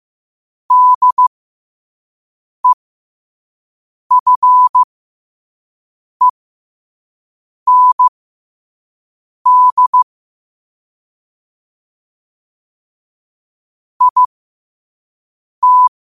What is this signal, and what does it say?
Morse code, heard as a keyed tone, about 15 wpm, spaced as for 5 wpm, 1000 Hz. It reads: DEFEND IT
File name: qotdfiles / 5wpm